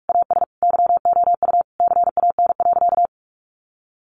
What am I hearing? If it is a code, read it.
Morse code, heard as a keyed tone, about 44 wpm, 700 Hz. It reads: AS YOU CAN.
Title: Sentences from Top 600 Words 45wpm